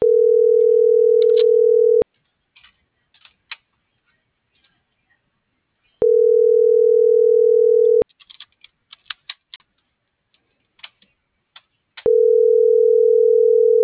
Ringback tone (North America), 440 + 480 Hz.